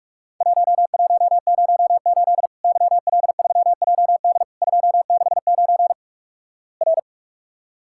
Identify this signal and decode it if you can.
Morse code, heard as a keyed tone, about 48 wpm, 700 Hz. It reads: 1108 YL3JD 269 R
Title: Worked All Europe DX Contest - QTC Exchange - 45wpm